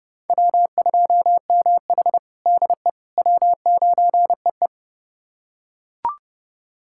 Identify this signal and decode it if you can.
Morse code, heard as a keyed tone, about 30 wpm, 700 Hz. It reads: W2MH DE W9EE
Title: USA Callsign DE Callsign - 4 Characters - 30wpm